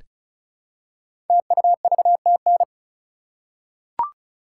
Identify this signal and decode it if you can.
Morse code, heard as a keyed tone, about 36 wpm, 700 Hz. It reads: TUVTN
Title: Code Groups - 5 Letters - 35wpm